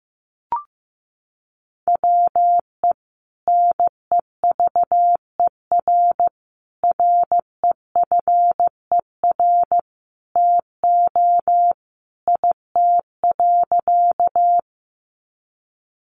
Morse code, heard as a keyed tone, about 15 wpm, 700 Hz. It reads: WE NEVER REFER TO IT.
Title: Sentences from Top 1000 Words 15wpm